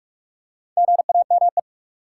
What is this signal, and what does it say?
Morse code, heard as a keyed tone, about 46 wpm, 700 Hz. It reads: GAME
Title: Four Letter Words - 45wpm